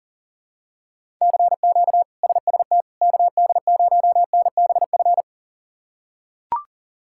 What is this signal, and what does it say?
Morse code, heard as a keyed tone, about 37 wpm, 700 Hz. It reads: CQ SST KD0NBF
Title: K1USN SST QSO Practice - 40wpm